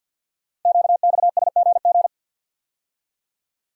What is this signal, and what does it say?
Morse code, heard as a keyed tone, about 52 wpm, 700 Hz. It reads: QXSGG
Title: Code Groups - 5 Letter:Numbers - 50wpm